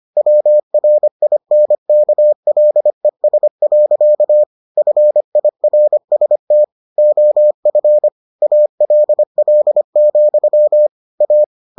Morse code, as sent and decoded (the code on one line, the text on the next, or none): .-- .-. .. -. -.- .-.. ... .-.-.- / ..-. .. .-. ... - / --- ..-. / .- .-.. .-.. --..-- / .-
WRINKLES. FIRST OF ALL, A